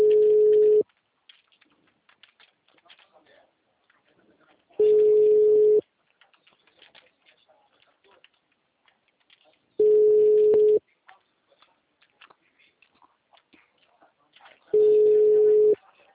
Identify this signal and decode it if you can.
Ringback tone (Europe, 425 Hz), with speech.